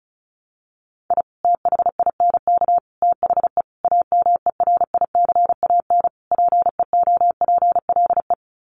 Morse code, heard as a keyed tone, about 34 wpm, 700 Hz. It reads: I THINK THE AMERICAN PEOPLE
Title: Sets of 5 Words Encore 35wpm